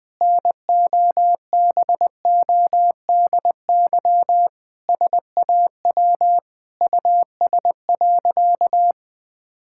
Morse code, heard as a keyed tone, about 21 wpm, 700 Hz. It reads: NOBODY SAW US.
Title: Sentences from Top 1600 Words 20wpm